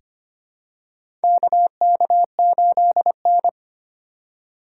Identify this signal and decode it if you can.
Morse code, heard as a keyed tone, about 25 wpm, 700 Hz. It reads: KK8N
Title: USA Call Signs - 3x Character Spacing - 25wpm